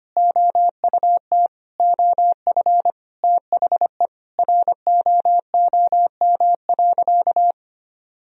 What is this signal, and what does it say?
Morse code, heard as a keyed tone, about 25 wpm, 700 Hz. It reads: OUT OF THE ROOM.
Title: Sentences from Top 500 Words and Abbreviations 25wpm